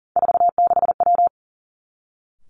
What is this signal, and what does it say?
Morse code, heard as a keyed tone, about 41 wpm, 700 Hz. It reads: 46W
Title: Three Challenging Letter-Number Combinations 40wpm